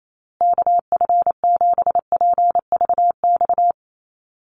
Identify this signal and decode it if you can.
Morse code, heard as a keyed tone, about 28 wpm, 700 Hz. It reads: KF7PVX
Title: USA Call Signs Repeated 3x 28wpm